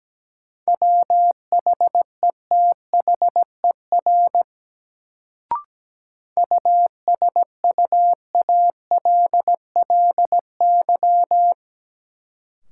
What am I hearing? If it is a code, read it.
Morse code, heard as a keyed tone, about 17 wpm, 700 Hz. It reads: WHETHER USUALLY
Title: Sentences from Top 1000 Words - Review Part 3 - New Word Drill - 17wpm